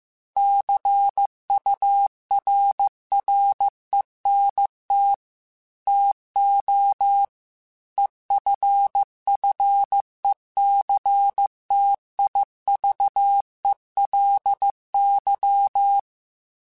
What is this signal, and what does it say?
Morse code, heard as a keyed tone, about 15 wpm, 800 Hz. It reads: CURRENT TO EFFECTIVELY